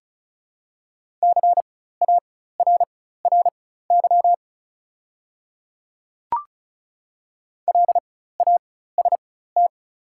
Morse code, heard as a keyed tone, about 36 wpm, spaced as for 16 wpm, 690 Hz. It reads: CARRY LAST
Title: 4x Character Spacing - Top 300 Words - 35wpm